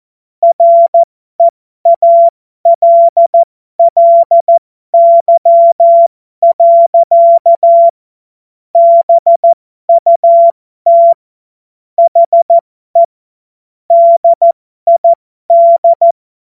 Morse code, heard as a keyed tone, about 14 wpm, spaced as for 12 wpm, 680 Hz. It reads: REALLY. BUT HE DID
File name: Morsekurs / Witze